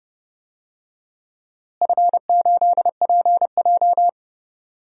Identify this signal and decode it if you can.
Morse code, heard as a keyed tone, about 31 wpm, 700 Hz. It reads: F8PJ
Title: DXCC-100 Call Signs - 4 Characters - Repeated 2x - 30wpm